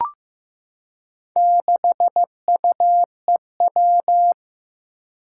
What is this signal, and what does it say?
Morse code, heard as a keyed tone, about 15 wpm, 700 Hz. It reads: 6UEW
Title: Four Challenging Letter-Number Combinations 15wpm